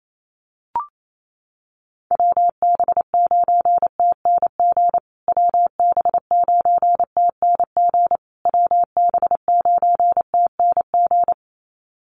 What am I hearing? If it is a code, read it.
Morse code, heard as a keyed tone, about 28 wpm, 700 Hz. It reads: WB9TNG WB9TNG WB9TNG
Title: USA Call Signs Repeated 3x 28wpm